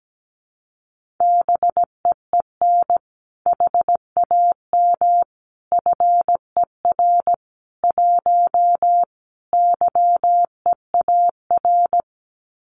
Morse code, heard as a keyed tone, about 17 wpm, 700 Hz. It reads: BEEN HAM FER 1 YEAR
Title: Common QSO Phrases Encore 17wpm